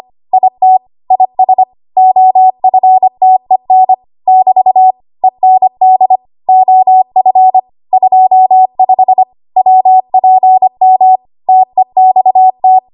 Morse code, heard as a keyed tone, about 25 wpm, 750 Hz. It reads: IT IS OFTEN <BT> END OF 25 WPM TEXT
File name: Archive / 25 WPM / 25WPM